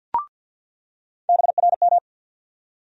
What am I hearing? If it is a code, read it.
Morse code, heard as a keyed tone, about 47 wpm, 700 Hz. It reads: BRM
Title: USA Call Sign Postfixes - 3 Letters 50wpm